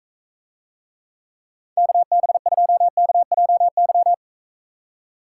Morse code, heard as a keyed tone, about 45 wpm, 700 Hz. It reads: KB2KJY